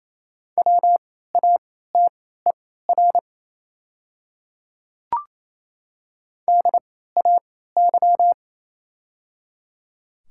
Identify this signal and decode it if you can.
Morse code, heard as a keyed tone, about 28 wpm, spaced as for 16 wpm, 690 Hz. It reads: WATER DAY
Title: 3x Character Spacing - Top 100 Words - 28wpm